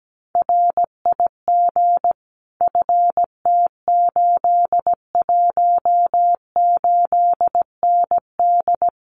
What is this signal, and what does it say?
Morse code, heard as a keyed tone, about 17 wpm, 700 Hz. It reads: RIG FT818ND
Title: Common QSO Phrases Encore 17wpm